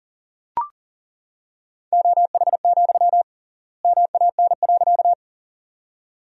Morse code, heard as a keyed tone, about 40 wpm, 700 Hz. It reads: OH, MAN.
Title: 3x Word-Spacing - Sentences from Top 500 Words - 40wpm